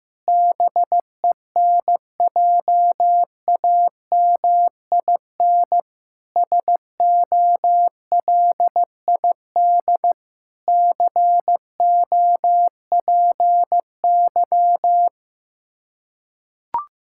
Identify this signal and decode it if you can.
Morse code, heard as a keyed tone, about 15 wpm, 700 Hz. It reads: BENJAMIN SOLID COPY
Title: K1USN SST QSO Practice 15wpm